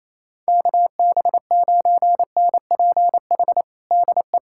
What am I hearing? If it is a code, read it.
Morse code, heard as a keyed tone, about 28 wpm, 700 Hz. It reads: KB9NPH DE
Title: USA Callsign DE Callsign 28wpm